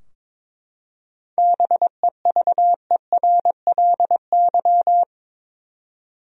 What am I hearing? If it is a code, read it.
Morse code, heard as a keyed tone, about 22 wpm, 700 Hz. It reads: BEVERLY
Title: USA Common Names - 22wpm